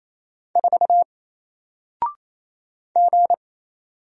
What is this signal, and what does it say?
Morse code, heard as a keyed tone, about 28 wpm, 700 Hz. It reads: 4 G
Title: Single Character (Letter:Number:Punctuation:Prosign) 28wpm